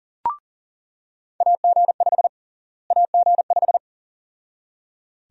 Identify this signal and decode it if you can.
Morse code, heard as a keyed tone, about 37 wpm, 700 Hz. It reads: AG5 AG5
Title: USA Call Sign Prefixes - Repeated 2x - 40wpm